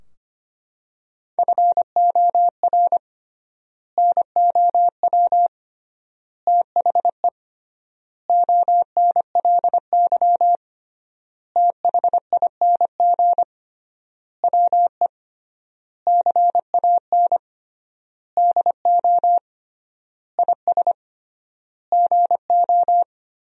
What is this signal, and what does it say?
Morse code, heard as a keyed tone, about 25 wpm, 700 Hz. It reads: FOR NOW THE ONLY THING WE CAN DO IS GO
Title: 3x Word-Spacing - Sentences from Top 200 Words - 25wpm